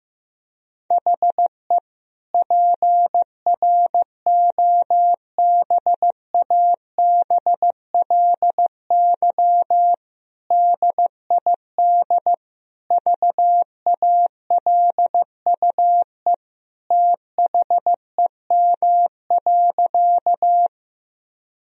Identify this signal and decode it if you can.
Morse code, heard as a keyed tone, about 15 wpm, 700 Hz. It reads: HE PROBABLY DID VALUE THEM.
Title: Sentences from Top 1000 Words - Review Part 4 - 15wpm